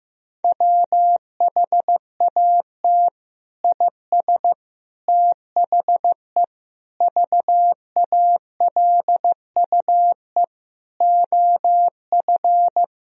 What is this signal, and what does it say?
Morse code, heard as a keyed tone, about 15 wpm, 700 Hz. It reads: WHAT IS THE VALUE OF
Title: Sentences from Top 1000 Words - Review Part 3 - 15wpm